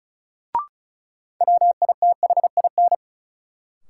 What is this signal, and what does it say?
Morse code, heard as a keyed tone, about 36 wpm, 700 Hz. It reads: WITHIN
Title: Sets of 2 Words - New Word Drill - 35wpm